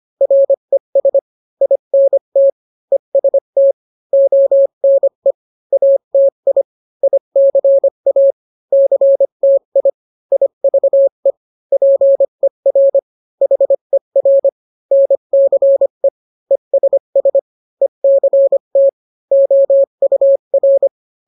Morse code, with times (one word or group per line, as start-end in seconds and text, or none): 0.2-1.2 RES
1.6-2.5 INT
2.9-3.7 EST
4.1-5.3 ONE
5.7-6.6 ATI
7.0-8.3 ICA
8.7-9.9 CTI
10.3-11.3 IVE
11.7-13.0 PER
13.4-14.5 HER
14.9-16.1 NCE
16.5-17.4 ESS
17.8-18.9 ECT
19.3-20.9 OUR